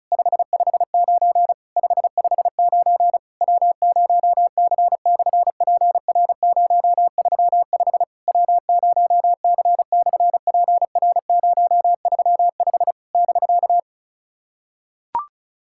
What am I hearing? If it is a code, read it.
Morse code, heard as a keyed tone, about 33 wpm, 700 Hz. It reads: 559 559 W0C/PR035 W0C/PR035 <BK>
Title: SOTA QSO Practice 35wpm